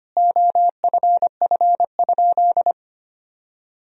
Morse code, heard as a keyed tone, about 25 wpm, 700 Hz. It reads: OFF?
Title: Sentences from Top 500 Words and Abbreviations 25wpm